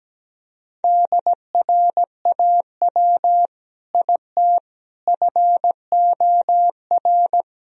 Morse code, heard as a keyed tone, about 17 wpm, 700 Hz. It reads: DRAW IT FOR
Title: Sentences from Top 1000 Words - Review Part 1 - 17wpm